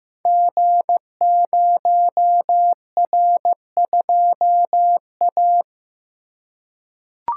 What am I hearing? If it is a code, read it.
Morse code, heard as a keyed tone, about 15 wpm, 700 Hz. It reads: G0R2A
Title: Code Groups - 5 Letter:Numbers - 15wpm